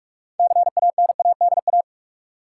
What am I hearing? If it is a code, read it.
Morse code, heard as a keyed tone, about 46 wpm, 700 Hz. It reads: CANADA
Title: Sets of 5 Words - New Word Drill - 45wpm